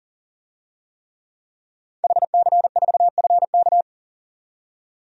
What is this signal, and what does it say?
Morse code, heard as a keyed tone, about 40 wpm, 700 Hz. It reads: HC4FK
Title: DXCC-100 Call Signs - 5 Characters - 40wpm